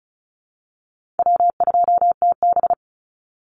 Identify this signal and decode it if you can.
Morse code, heard as a keyed tone, about 36 wpm, 700 Hz. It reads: W2TB
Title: Call Signs - 35wpm